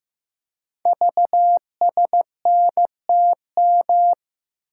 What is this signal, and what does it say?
Morse code, heard as a keyed tone, about 15 wpm, 700 Hz. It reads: VSNTM
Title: Code Groups - 5 Letter:Numbers - 15wpm